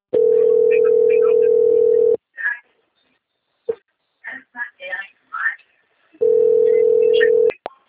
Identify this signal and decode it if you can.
Ringback tone (North America), 440 + 480 Hz, with speech.
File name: out- - - - - .25096